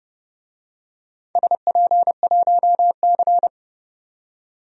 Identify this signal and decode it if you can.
Morse code, heard as a keyed tone, about 30 wpm, 700 Hz. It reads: SP1C
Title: DXCC-100 Call Signs - 4 Characters - 30wpm